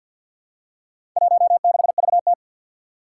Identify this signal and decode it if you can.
Morse code, heard as a keyed tone, about 48 wpm, 700 Hz. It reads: 16VT
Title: Four Challenging Letter-Number Combinations 50wpm